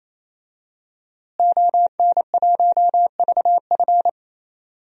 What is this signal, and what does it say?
Morse code, heard as a keyed tone, about 28 wpm, 700 Hz. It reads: ON1VF
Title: DXCC-100 Call Signs - 5 Characters - Repeated 3x - 28wpm